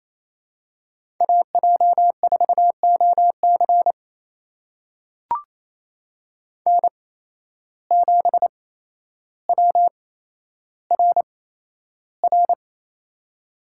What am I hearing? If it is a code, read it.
Morse code, heard as a keyed tone, about 28 wpm, spaced as for 12 wpm, 700 Hz. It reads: AJ4OC N7WRR
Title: USA Call Signs - 8x Character Spacing - 28wpm